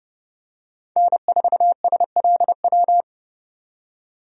Morse code, heard as a keyed tone, about 32 wpm, 700 Hz. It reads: N4SLW
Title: USA Call Signs - 5 Characters - 30wpm